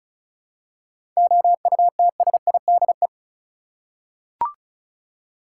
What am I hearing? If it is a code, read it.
Morse code, heard as a keyed tone, about 36 wpm, 700 Hz. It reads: OUTSIDE
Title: Seven Letter Words - 35wpm